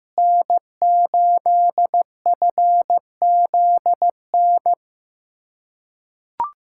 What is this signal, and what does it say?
Morse code, heard as a keyed tone, about 15 wpm, 700 Hz. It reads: N8FZN
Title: USA Callsign DE Callsign - 5 Characters - 15wpm